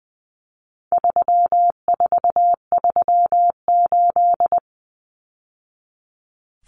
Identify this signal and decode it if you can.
Morse code, heard as a keyed tone, about 20 wpm, 700 Hz. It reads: 3438
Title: Numbers - 4-Digit 20wpm